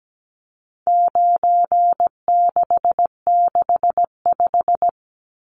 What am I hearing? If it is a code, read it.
Morse code, heard as a keyed tone, about 17 wpm, 700 Hz. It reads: 9665
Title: Numbers - 4-Digit 17wpm